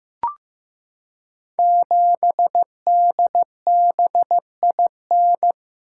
Morse code, heard as a keyed tone, about 15 wpm, 700 Hz. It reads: 7DBIN
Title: Code Groups - 5 Letter:Numbers - 15wpm